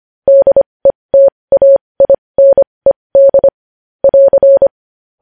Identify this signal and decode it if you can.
Morse code, heard as a keyed tone, about 25 wpm, spaced as for 20 wpm, 550 Hz. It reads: DETAINED <AR>
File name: podcast / News headlines-1x-25x20